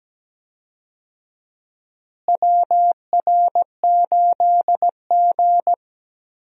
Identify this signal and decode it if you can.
Morse code, heard as a keyed tone, about 17 wpm, 700 Hz. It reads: WR8G
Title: USA Call Signs - 4 Characters - Repeated 2x - 17wpm